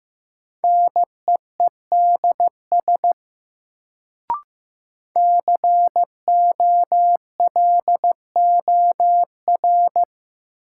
Morse code, heard as a keyed tone, about 16 wpm, 700 Hz. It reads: NEEDS COLOR
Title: Sentences from Top 1600 Words - New Word Drill Alt - 15wpm